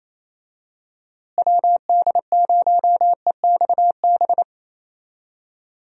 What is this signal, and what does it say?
Morse code, heard as a keyed tone, about 28 wpm, 700 Hz. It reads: WD0EXB
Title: USA Call Signs - 6 Characters - Repeated 2x - 28wpm